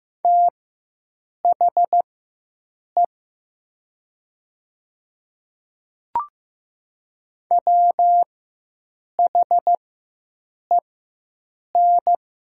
Morse code, heard as a keyed tone, about 15 wpm, spaced as for 7 wpm, 700 Hz. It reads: THE WHEN
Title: 4x Character Spacing - Top 100 Words - 15wpm